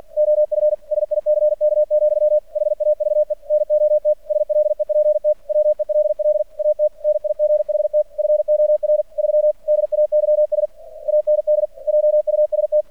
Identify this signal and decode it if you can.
Morse code, heard as a keyed tone, about 50 wpm, 600 Hz. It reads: OF SICK, HAVE NOT SLEPT WELL AT NIGHT FOR 3 DAYS AND JUST
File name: archived / CRSnet